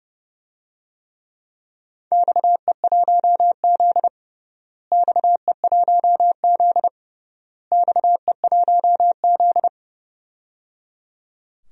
Morse code, heard as a keyed tone, about 30 wpm, 700 Hz. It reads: XE1Z XE1Z XE1Z
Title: DXCC-100 Call Signs - 4 Characters - Repeated 3x - 30wpm